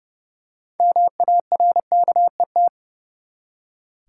Morse code, heard as a keyed tone, about 30 wpm, 700 Hz. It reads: MARKET
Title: Six Letter Words - 30wpm